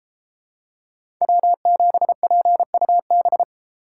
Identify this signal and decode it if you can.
Morse code, heard as a keyed tone, about 32 wpm, 700 Hz. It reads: W7PUB